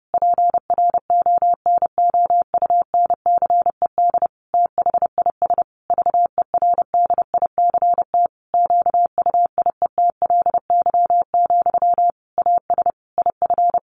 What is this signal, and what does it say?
Morse code, heard as a keyed tone, about 28 wpm, 700 Hz. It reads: PRONOUNCED THIS VERDICT QUIETLY, AS IF
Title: A Sound Of Thunder 30wpm